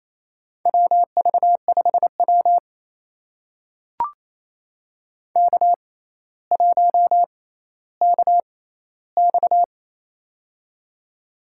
Morse code, heard as a keyed tone, about 28 wpm, spaced as for 14 wpm, 700 Hz. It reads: WV5W K1KX
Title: USA Call Signs - 4 Characters - 6x Character Spacing - 28wpm